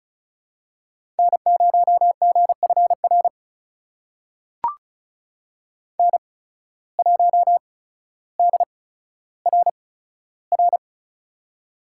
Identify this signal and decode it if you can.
Morse code, heard as a keyed tone, about 36 wpm, spaced as for 15 wpm, 700 Hz. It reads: N0GFR N1DRR
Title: USA Call Signs - 5 Characters - 8x Character Spacing - 35wpm